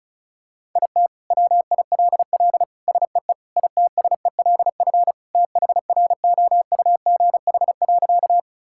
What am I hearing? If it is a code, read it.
Morse code, heard as a keyed tone, about 34 wpm, 700 Hz. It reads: IT WILL SEE ITSELF THROUGH.
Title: Sentences from Top 1000 Words - Review Part 5 - 35wpm